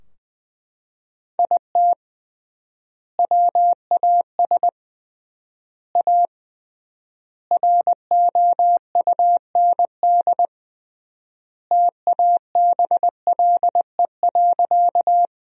Morse code, heard as a keyed tone, about 20 wpm, 700 Hz. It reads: IT WAS A ROUND TABLE.
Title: 3x Word-Spacing - Sentences from Top 400 Words - 20wpm